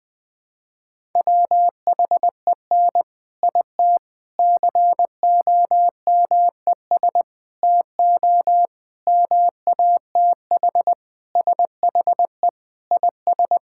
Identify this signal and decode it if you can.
Morse code, heard as a keyed tone, about 20 wpm, 700 Hz. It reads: WHEN IT COMES TO MATH SHE IS
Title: Sentences with 4-Word Phrases - 20wpm